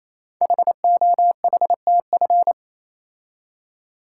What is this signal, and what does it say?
Morse code, heard as a keyed tone, about 29 wpm, 700 Hz. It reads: HOHTF
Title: Code Groups - 5 Letters - 28wpm